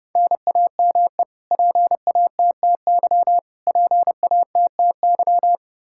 Morse code, heard as a keyed tone, about 32 wpm, 700 Hz. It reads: NAME PATTY PATTY
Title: Practice QSO - Exchange 1 - Beginning - 30wpm